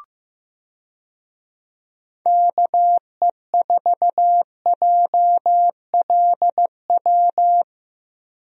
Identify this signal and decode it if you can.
Morse code, heard as a keyed tone, about 15 wpm, 700 Hz. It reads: KE4JLW
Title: USA Call Signs - 6 Characters - Repeated 2x - 15wpm